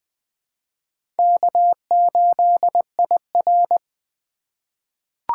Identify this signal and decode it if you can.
Morse code, heard as a keyed tone, about 20 wpm, 700 Hz. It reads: K8IR
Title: USA Call Signs - 4 Characters - 4x Character Spacing - 20wpm